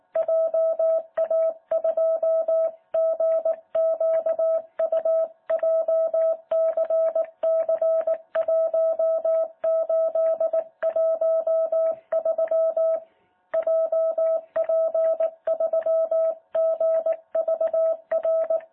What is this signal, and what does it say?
Morse code, heard as a keyed tone, about 18 wpm, 650 Hz. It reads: JA2GQU JCC1813 JP3GVR
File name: jh8cbh / 2021cw